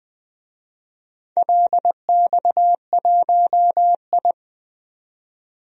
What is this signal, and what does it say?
Morse code, heard as a keyed tone, about 20 wpm, 700 Hz. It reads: LX1I
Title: DXCC-100 Call Signs - 4 Characters - 20wpm